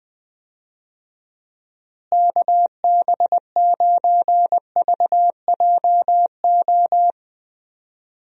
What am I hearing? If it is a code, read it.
Morse code, heard as a keyed tone, about 20 wpm, 700 Hz. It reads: KB9VJO
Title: USA Call Signs - 6 Characters - Repeated 3x - 20wpm